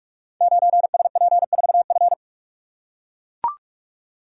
Morse code, heard as a keyed tone, about 44 wpm, 700 Hz. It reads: 9SP4F
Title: Code Groups - 5 Letter:Numbers - 45wpm